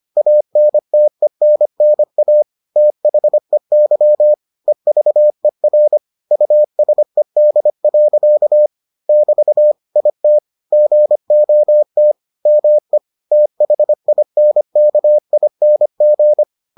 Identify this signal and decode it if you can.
Morse code, heard as a keyed tone, about 25 wpm, 600 Hz. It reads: ANTENNA THEY EVER USED. <BT> IT GOT ME THINKING